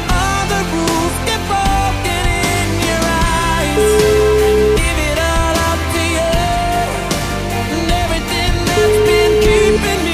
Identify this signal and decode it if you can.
Ringback tone (Europe, 425 Hz), with noise.